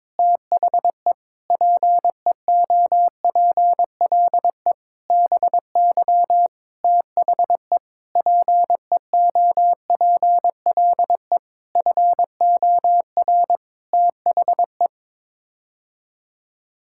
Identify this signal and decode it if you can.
Morse code, heard as a keyed tone, about 22 wpm, 700 Hz. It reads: THE PEOPLE BY THE PEOPLE FOR THE
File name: Sets of 9 Words - 22wpm